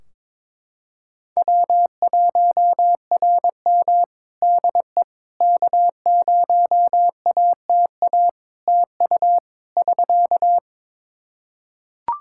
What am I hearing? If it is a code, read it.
Morse code, heard as a keyed tone, about 22 wpm, 700 Hz. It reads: W1RM DE K0ATA TU <SK>
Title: Practice QSO - Exchange 4 - Ending - 22wpm